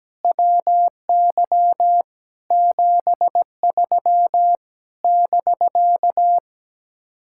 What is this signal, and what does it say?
Morse code, heard as a keyed tone, about 17 wpm, 700 Hz. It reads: WY 73 <BK>
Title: Speed-Racing Common QSO Phrases Encore - 17wpm